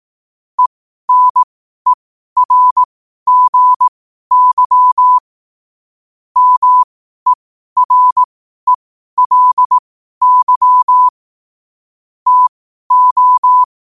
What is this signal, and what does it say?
Morse code, heard as a keyed tone, about 18 wpm, spaced as for 12 wpm, 1000 Hz. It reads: ENERGY MERELY TO
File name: qotdfiles / 12wpm